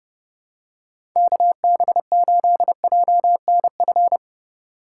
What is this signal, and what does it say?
Morse code, heard as a keyed tone, about 30 wpm, 700 Hz. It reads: KB8JNF
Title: USA Call Signs - 6 Characters - 2x Character Spacing - 30wpm